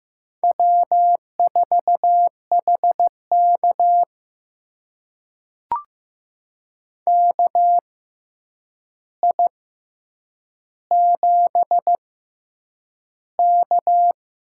Morse code, heard as a keyed tone, about 15 wpm, spaced as for 8 wpm, 700 Hz. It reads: W4HK KI7K
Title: USA Call Signs - 6x Character Spacing - 15wpm